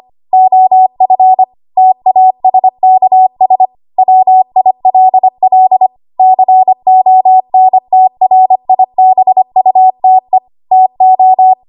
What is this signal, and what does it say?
Morse code, heard as a keyed tone, about 25 wpm, 750 Hz. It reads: OF TASKS WILL CONTRIBUTE TO
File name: Morse / 25WPM